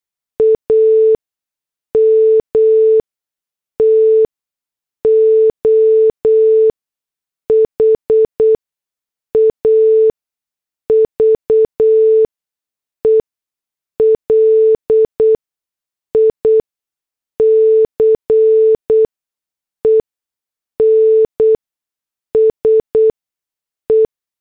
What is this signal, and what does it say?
Morse code, heard as a keyed tone, about 8 wpm, spaced as for 6 wpm, 440 Hz. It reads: AM TO HAVE LICENSE
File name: morse / headlines